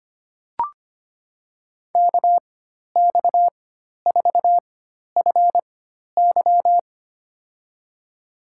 Morse code, heard as a keyed tone, about 25 wpm, spaced as for 12 wpm, 700 Hz. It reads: KX4FY
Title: USA Call Signs - 4x Character Spacing - 25wpm